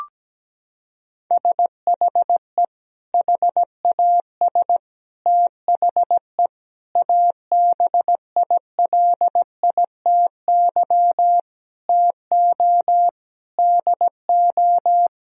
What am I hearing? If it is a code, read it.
Morse code, heard as a keyed tone, about 17 wpm, 700 Hz. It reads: SHE HAS THE ABILITY TO DO
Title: Sentences from Top 1000 Words - Review Part 3 - 17wpm